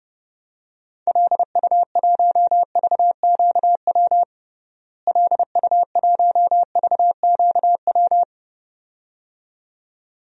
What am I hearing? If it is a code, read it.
Morse code, heard as a keyed tone, about 32 wpm, 700 Hz. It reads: LU1VQW LU1VQW
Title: DXCC-100 Call Signs - 6 Characters - Repeated 2x - 30wpm